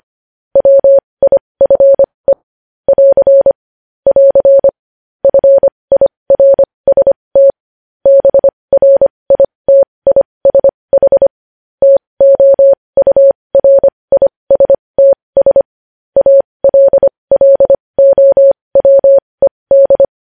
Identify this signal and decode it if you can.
Morse code, heard as a keyed tone, about 25 wpm, spaced as for 20 wpm, 550 Hz. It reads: WIFE <AR> <AR> FIRST BRITISH TOURISTS ALLOWED
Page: News Headlines in Morse Code at 20 WPM